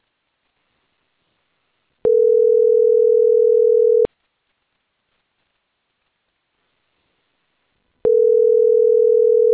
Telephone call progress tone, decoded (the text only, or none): ringback tone